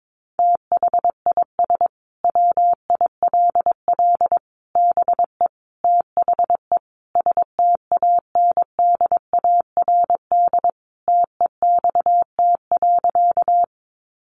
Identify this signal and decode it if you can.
Morse code, heard as a keyed tone, about 22 wpm, 700 Hz. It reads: THIS WILL BE THE STANDARD TEXT.
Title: Sentences from Top 1200 Words 22wpm